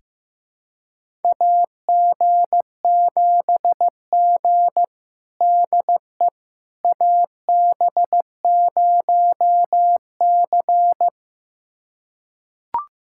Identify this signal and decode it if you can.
Morse code, heard as a keyed tone, about 15 wpm, 700 Hz. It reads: AG7G DE AB0C
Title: USA Callsign DE Callsign - 4 Characters - 15wpm